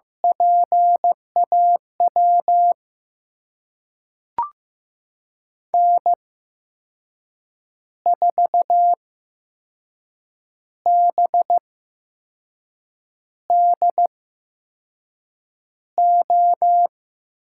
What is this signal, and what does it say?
Morse code, heard as a keyed tone, about 15 wpm, spaced as for 5 wpm, 700 Hz. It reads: PAW N4BDO
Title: USA Call Signs - 5 Characters - 8x Character Spacing - 15wpm